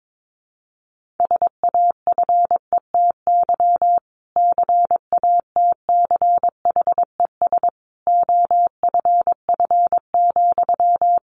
Morse code, heard as a keyed tone, about 22 wpm, 700 Hz. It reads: SAFETY CATCHES OFF,
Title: A Sound Of Thunder 22wpm